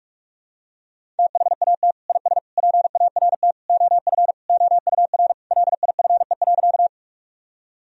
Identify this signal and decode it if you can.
Morse code, heard as a keyed tone, about 42 wpm, 700 Hz. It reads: THAT IS PART OF OUR LIFE.
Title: Sentences from Top 1000 Words - Review Part 1 - 45wpm